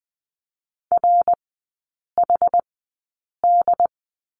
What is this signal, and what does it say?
Morse code, heard as a keyed tone, about 20 wpm, spaced as for 8 wpm, 700 Hz. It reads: RHD
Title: 138 Lesson 23e - Letter f - Three Characters 20wpm